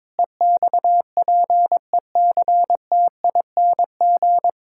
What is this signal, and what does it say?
Morse code, heard as a keyed tone, about 22 wpm, 700 Hz. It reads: EXPECTING
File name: practice / 22wpm / Sentences with 8-Word Phrases - 22wpm